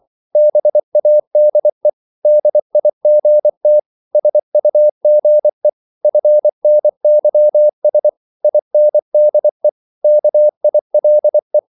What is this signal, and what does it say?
Morse code, heard as a keyed tone, about 24 wpm, 600 Hz. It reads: BADE DIGT SUGE FNYS INDE KILE
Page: Korte danske ord 24wpm